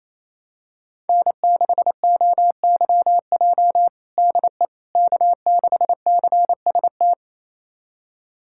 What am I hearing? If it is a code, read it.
Morse code, heard as a keyed tone, about 28 wpm, 700 Hz. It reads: N6OYJ DE K6CST
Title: USA Callsign DE Callsign - 5 Characters - 28wpm